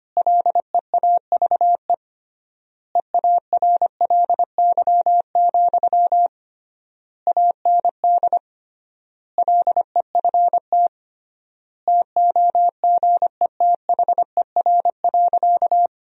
Morse code, heard as a keyed tone, about 25 wpm, 700 Hz. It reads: LEAVE EARLY, AND LEFT TOGETHER.
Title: 3x Word-Spacing - Sentences from Top 400 Words - 25wpm